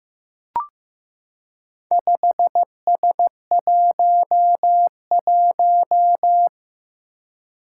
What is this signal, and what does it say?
Morse code, heard as a keyed tone, about 15 wpm, 700 Hz. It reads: 5S11
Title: Four Challenging Letter-Number Combinations 15wpm